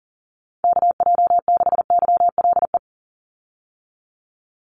Morse code, heard as a keyed tone, about 40 wpm, 700 Hz. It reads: KJ6YLE
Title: USA Call Signs 40wpm